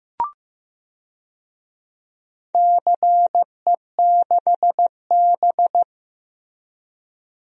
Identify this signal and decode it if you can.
Morse code, heard as a keyed tone, about 15 wpm, 700 Hz. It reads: CE6B
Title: DXCC-100 Call Signs - 4 Characters - Repeated 2x - 15wpm